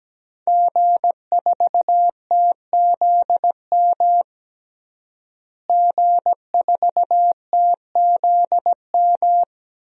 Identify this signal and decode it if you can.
Morse code, heard as a keyed tone, about 17 wpm, 700 Hz. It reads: G4TZM G4TZM
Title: DXCC-100 Call Signs - 5 Characters - Repeated 2x - 17wpm